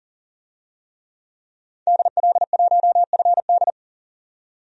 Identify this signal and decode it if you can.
Morse code, heard as a keyed tone, about 39 wpm, 700 Hz. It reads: DL1FD
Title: DXCC-100 Call Signs - 5 Characters - 40wpm